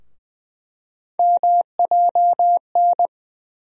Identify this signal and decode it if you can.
Morse code, heard as a keyed tone, about 20 wpm, 700 Hz. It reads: MJN
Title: USA Call Sign Postfixes - 3 Letters 20wpm